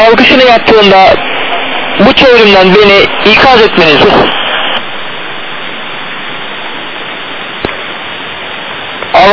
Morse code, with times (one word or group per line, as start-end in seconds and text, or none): none